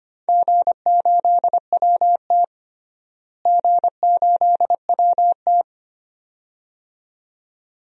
Morse code, heard as a keyed tone, about 25 wpm, 700 Hz. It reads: G8WT G8WT
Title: DXCC-100 Call Signs - 4 Characters - Repeated 2x - 25wpm